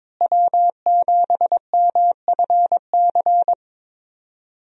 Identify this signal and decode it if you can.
Morse code, heard as a keyed tone, about 22 wpm, 700 Hz. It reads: W7MFC
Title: USA Call Signs - 5 Characters - Repeated 3x - 22wpm